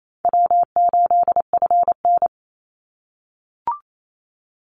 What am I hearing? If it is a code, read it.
Morse code, heard as a keyed tone, about 28 wpm, 700 Hz. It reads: W8FN
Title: Easy Call Signs - 28wpm